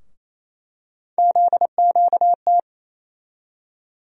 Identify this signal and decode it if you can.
Morse code, heard as a keyed tone, about 28 wpm, 700 Hz. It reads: ZQT
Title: USA Call Sign Postfixes - 3 Letters 28wpm